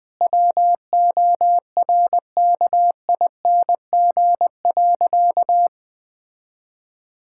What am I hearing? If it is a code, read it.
Morse code, heard as a keyed tone, about 20 wpm, 700 Hz. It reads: WORKING.
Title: Sentences with 5-Word Phrases - 20wpm